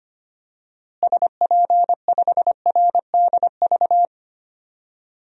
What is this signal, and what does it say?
Morse code, heard as a keyed tone, about 25 wpm, 700 Hz. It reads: SP5RDV
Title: DXCC-100 Call Signs - 6 Characters - Repeated 3x - 25wpm